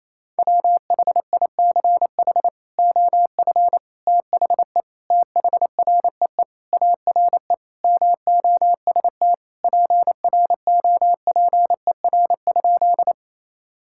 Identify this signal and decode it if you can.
Morse code, heard as a keyed tone, about 28 wpm, 700 Hz. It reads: WHICH OF THE THREE ARE MOST PROPER?
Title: Sentences from Top 1000 Words - Review Part 5 - 28wpm